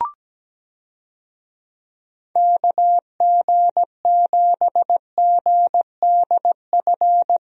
Morse code, heard as a keyed tone, about 17 wpm, 700 Hz. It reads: KG7GDF
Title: USA Call Signs - 6 Characters - Repeated 3x - 17wpm